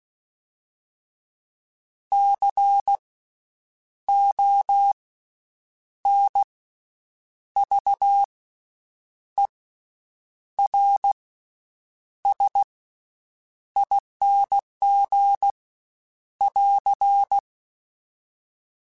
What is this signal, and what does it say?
Morse code, heard as a keyed tone, about 16 wpm, spaced as for 7 wpm, 800 Hz. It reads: CONVERSING <AR>
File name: Special INGs / Special INGs-mp3s / VerySlowToFaster Ing @15